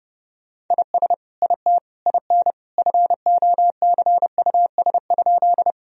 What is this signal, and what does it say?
Morse code, heard as a keyed tone, about 32 wpm, 700 Hz. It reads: IS IT IN FOCUS?
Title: Sentences from Top 1000 Words - Review Part 4 - 30wpm